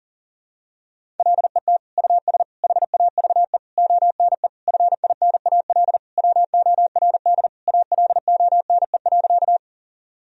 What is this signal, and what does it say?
Morse code, heard as a keyed tone, about 37 wpm, 700 Hz. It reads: LET US HAVE ONE FINAL WORD ALONE.
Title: Sentences from Top 1000 Words - Review Part 2 - 40wpm